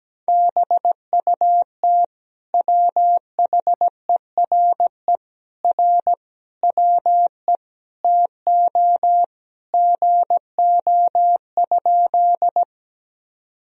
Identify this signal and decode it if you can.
Morse code, heard as a keyed tone, about 17 wpm, 700 Hz. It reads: BUT WHERE R WE TO GO?
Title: Sentences from Top 200 Words and Abbreviations 17wpm